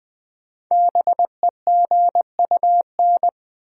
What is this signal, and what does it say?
Morse code, heard as a keyed tone, about 20 wpm, 700 Hz. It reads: BEGUN
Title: Sentences from Top 1000 Words - Review Part 6 - New Word Drill - 20wpm